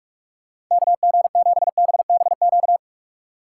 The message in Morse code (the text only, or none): KG7BBQ